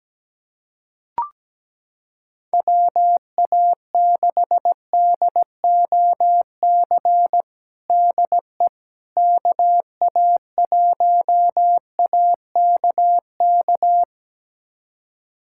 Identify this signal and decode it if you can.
Morse code, heard as a keyed tone, about 17 wpm, 700 Hz. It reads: WA6DOC DE KA1AKK
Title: USA Callsign DE Callsign - 6 Characters - 17wpm